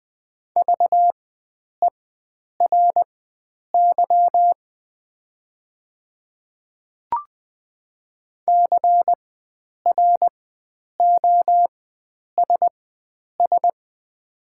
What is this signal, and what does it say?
Morse code, heard as a keyed tone, about 20 wpm, spaced as for 9 wpm, 700 Hz. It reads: VERY CROSS
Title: 4x Character Spacing - Top 500 Words - 20wpm